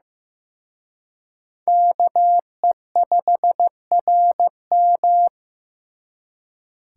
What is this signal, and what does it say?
Morse code, heard as a keyed tone, about 18 wpm, 700 Hz. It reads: KE5RM
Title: Speed Racing - USA 2x2 Call Signs 15wpm